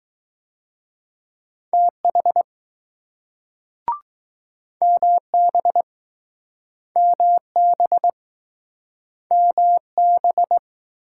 Morse code, heard as a keyed tone, about 20 wpm, 700 Hz. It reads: TH MB MB MB